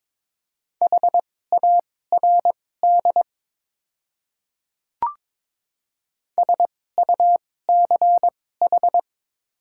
Morse code, heard as a keyed tone, about 22 wpm, spaced as for 16 wpm, 700 Hz. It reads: HARD SUCH
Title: 2x Character Spacing - Top 300 Words - 22wpm